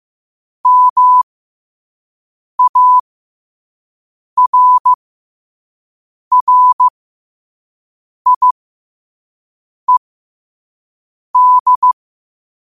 Morse code, heard as a keyed tone, about 15 wpm, spaced as for 5 wpm, 1000 Hz. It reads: MARRIED
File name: 5wpm